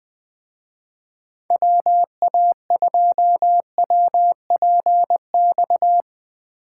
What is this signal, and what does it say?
Morse code, heard as a keyed tone, about 24 wpm, 700 Hz. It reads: WA2WPX
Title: Speed Racing - USA Call Signs 20wpm